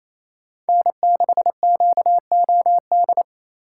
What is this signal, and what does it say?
Morse code, heard as a keyed tone, about 28 wpm, 700 Hz. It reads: N6QOD
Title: USA Call Signs - 3x Character Spacing - 28wpm